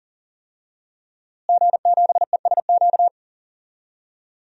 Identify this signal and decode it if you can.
Morse code, heard as a keyed tone, about 39 wpm, 700 Hz. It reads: G7ESQ
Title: DXCC-100 Call Signs - 5 Characters - Repeated 3x - 40wpm